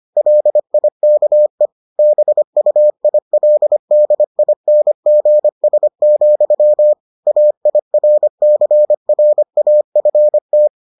Morse code, heard as a keyed tone, about 25 wpm, 600 Hz. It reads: LIKE BUILDINGS, AIRCRAFT